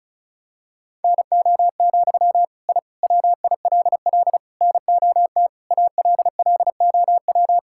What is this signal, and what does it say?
Morse code, heard as a keyed tone, about 36 wpm, 700 Hz. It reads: NO, I WILL NOT ALLOW
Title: Sentences from Top 1000 Words - Review Part 5 - 35wpm